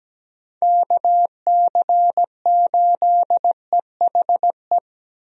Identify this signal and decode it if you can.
Morse code, heard as a keyed tone, about 17 wpm, 700 Hz. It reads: KC8EHE
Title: USA Call Signs - 6 Characters - Repeated 3x - 17wpm